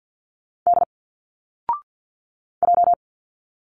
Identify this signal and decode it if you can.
Morse code, heard as a keyed tone, about 50 wpm, 700 Hz. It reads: D J
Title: Single Letters 50wpm